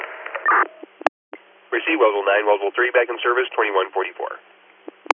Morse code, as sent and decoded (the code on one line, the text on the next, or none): none